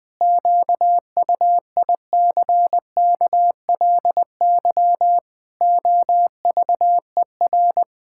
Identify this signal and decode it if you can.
Morse code, heard as a keyed tone, about 20 wpm, 700 Hz. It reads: QUICKLY OVER
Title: Pangram Challenge v2 - 20wpm